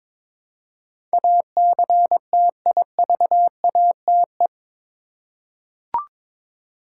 Morse code, heard as a keyed tone, about 22 wpm, 700 Hz. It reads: ACTIVATE
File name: practice / 22wpm / Eight Letter Words - 22wpm